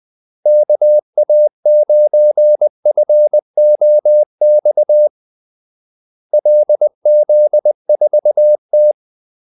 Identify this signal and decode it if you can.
Morse code, heard as a keyed tone, about 20 wpm, 600 Hz. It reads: KA9FOX LZ4T
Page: Kaldesignaler 20 wpm